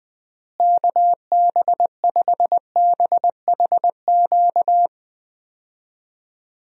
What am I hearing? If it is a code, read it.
Morse code, heard as a keyed tone, about 20 wpm, 700 Hz. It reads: KB5BHQ
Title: USA Call Signs - 6 Characters - Repeated 3x - 20wpm